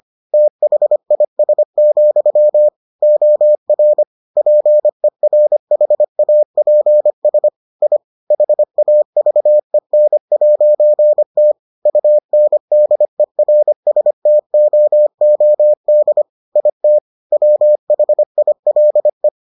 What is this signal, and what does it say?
Morse code, heard as a keyed tone, about 25 wpm, 600 Hz. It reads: THIS, OR PERHAPS I HAVEN'T UNDERSTOOD IT WHILE